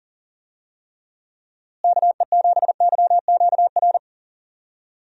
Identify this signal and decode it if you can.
Morse code, heard as a keyed tone, about 40 wpm, 700 Hz. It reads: KE7YQR
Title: USA Call Signs - 6 Characters - Repeated 2x - 40wpm